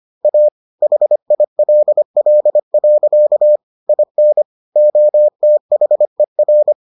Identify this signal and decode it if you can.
Morse code, heard as a keyed tone, about 25 wpm, 600 Hz. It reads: A HILL. IN OTHER